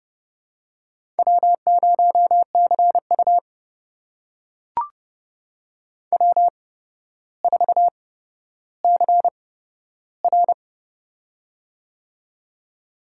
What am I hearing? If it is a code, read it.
Morse code, heard as a keyed tone, about 31 wpm, spaced as for 13 wpm, 700 Hz. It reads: W0CU W4CR
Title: USA Call Signs - 4 Characters - 8x Character Spacing - 30wpm